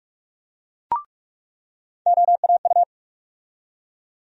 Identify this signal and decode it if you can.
Morse code, heard as a keyed tone, about 46 wpm, 700 Hz. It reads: OAU
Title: USA Call Sign Postfixes - 3 Letters 45wpm